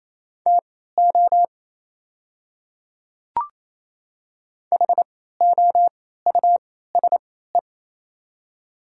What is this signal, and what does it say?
Morse code, heard as a keyed tone, about 28 wpm, spaced as for 16 wpm, 700 Hz. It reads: TO HOUSE
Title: 3x Character Spacing - Top 300 Words - 28wpm